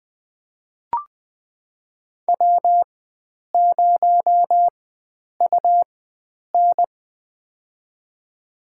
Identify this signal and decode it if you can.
Morse code, heard as a keyed tone, about 20 wpm, spaced as for 9 wpm, 700 Hz. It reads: W0UN
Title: USA Call Signs - 4 Characters - 4x Character Spacing - 20wpm